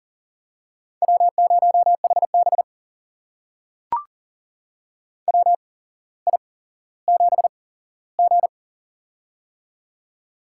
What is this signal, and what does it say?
Morse code, heard as a keyed tone, about 40 wpm, spaced as for 17 wpm, 700 Hz. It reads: W0HB WI7G
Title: USA Call Signs - 4 Characters - 8x Character Spacing - 40wpm